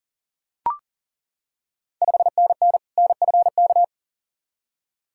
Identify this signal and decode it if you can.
Morse code, heard as a keyed tone, about 40 wpm, 690 Hz. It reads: 5NN NFK
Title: Ontario QSO Party Exchange 40wpm